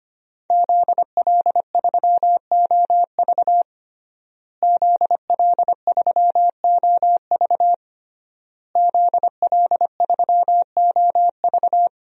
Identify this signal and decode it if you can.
Morse code, heard as a keyed tone, about 25 wpm, 700 Hz. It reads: ZL3OV ZL3OV ZL3OV
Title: DXCC-100 Call Signs - 5 Characters - Repeated 3x - 25wpm